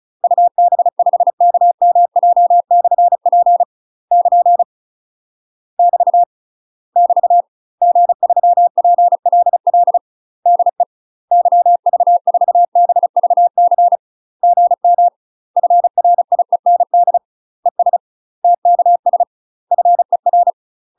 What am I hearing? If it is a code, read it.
Morse code, heard as a keyed tone, about 34 wpm, 700 Hz. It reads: UB5KMJ/P <KN> <BT> <BT> G3PLL DE YV4BVC GM FRIEND ES TKS FER
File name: UK MASTER QSOs TRAINING / 35WPM / UK MASTER QSOs 0037